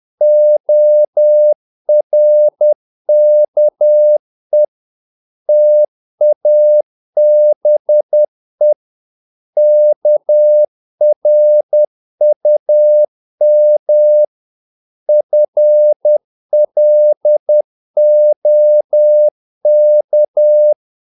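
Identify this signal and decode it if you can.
Morse code, heard as a keyed tone, about 10 wpm, 600 Hz. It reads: ORKE TABE KRUM FLOK
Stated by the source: Morse practice of short Danish words at 10 wpm, from CW med Gnister.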